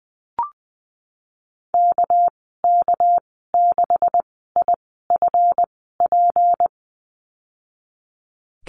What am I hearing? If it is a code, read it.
Morse code, heard as a keyed tone, about 19 wpm, spaced as for 14 wpm, 700 Hz. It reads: KK6IFP
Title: 207 Lesson 33g - Letter j - Call Signs 20wpm